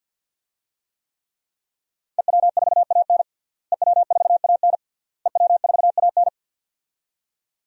Morse code, heard as a keyed tone, about 47 wpm, 700 Hz. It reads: EW4AN EW4AN EW4AN
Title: DXCC-100 Call Signs - 5 Characters - Repeated 3x - 50wpm